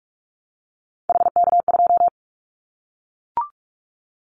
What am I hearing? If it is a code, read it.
Morse code, heard as a keyed tone, about 44 wpm, 700 Hz. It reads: HK2
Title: Callsign Prefixes for Top 100 DXCC - 45wpm